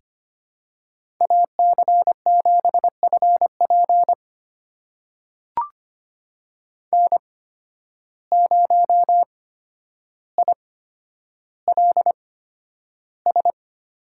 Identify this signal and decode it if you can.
Morse code, heard as a keyed tone, about 25 wpm, spaced as for 11 wpm, 700 Hz. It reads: AC7FP N0ILS
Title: USA Call Signs - 5 Characters - 8x Character Spacing - 25wpm